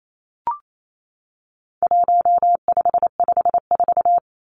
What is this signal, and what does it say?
Morse code, heard as a keyed tone, about 28 wpm, 700 Hz. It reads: 1554
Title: Numbers - 4-Digit 28wpm